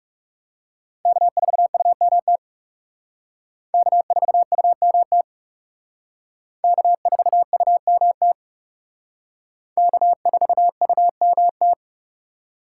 Morse code, heard as a keyed tone, about 39 wpm, 700 Hz. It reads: K4UMT K4UMT K4UMT K4UMT